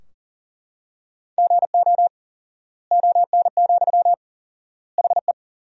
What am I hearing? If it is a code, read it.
Morse code, heard as a keyed tone, about 40 wpm, 700 Hz. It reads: GO ON, HE
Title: 4x Word-Spacing - Sentences from Top 100 Words - 40wpm